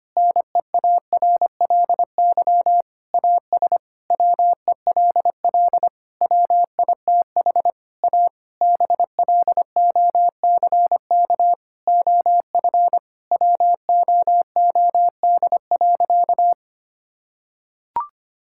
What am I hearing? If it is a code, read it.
Morse code, heard as a keyed tone, about 25 wpm, 700 Hz. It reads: NEARLY AS WELL WITH A BLOCK OF WOOD.
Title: Sentences from Top 1600 Words 25wpm